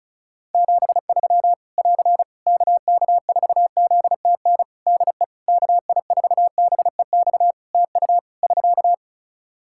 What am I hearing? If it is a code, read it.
Morse code, heard as a keyed tone, about 36 wpm, 700 Hz. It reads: 73 <AR> KK4ZTN DE KI4BEX TU <SK>
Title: K1USN SST QSO Practice 35wpm